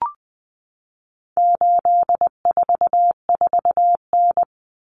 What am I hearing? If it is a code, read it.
Morse code, heard as a keyed tone, about 20 wpm, 700 Hz. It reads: 844N
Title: Common Cut Numbers - 4-Digit 20wpm